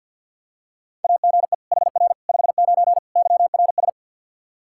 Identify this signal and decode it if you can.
Morse code, heard as a keyed tone, about 47 wpm, 700 Hz. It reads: AGE HR 59 YRS
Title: Practice QSO - Exchange 3 - More Info - 50wpm